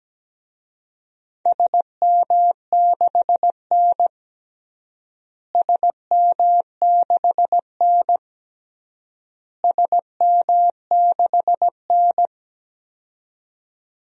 Morse code, heard as a keyed tone, about 17 wpm, 700 Hz. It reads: SM6N SM6N SM6N
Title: DXCC-100 Call Signs - 4 Characters - Repeated 3x - 17wpm